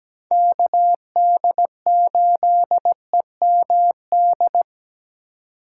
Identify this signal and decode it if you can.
Morse code, heard as a keyed tone, about 20 wpm, 700 Hz. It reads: KD8EMD
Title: Speed Racing - USA Call Signs 17wpm